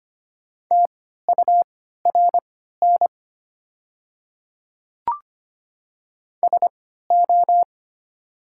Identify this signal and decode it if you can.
Morse code, heard as a keyed tone, about 25 wpm, spaced as for 14 wpm, 690 Hz. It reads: TURN SO
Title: 3x Character Spacing - Top 200 Words - 25wpm